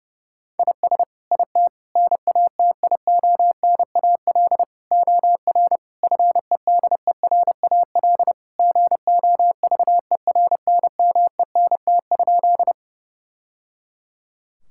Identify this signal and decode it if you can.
Morse code, heard as a keyed tone, about 32 wpm, 700 Hz. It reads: IS IT NATIONAL OR FEDERAL GOVERNMENT?
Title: Sentences from Top 1000 Words - Review Part 5 - 30wpm